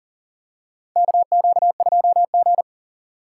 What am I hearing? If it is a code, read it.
Morse code, heard as a keyed tone, about 40 wpm, 700 Hz. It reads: KQ2G
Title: USA Call Signs - 4 Characters - 4x Character Spacing - 40wpm